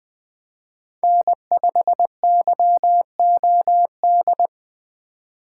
Morse code, heard as a keyed tone, about 20 wpm, 700 Hz. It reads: N5YOD
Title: Speed Racing - USA 1x3 Call Signs 17wpm